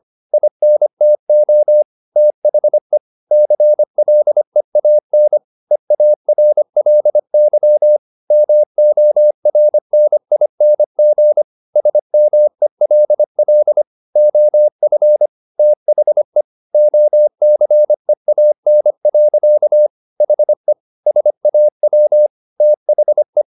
Morse code, heard as a keyed tone, about 25 wpm, 600 Hz. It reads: INTO THE CLEAN EARLY MORNING SMELL OF THE OCEAN. HE SAW THE
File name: the old man and the sea part 1 25wpm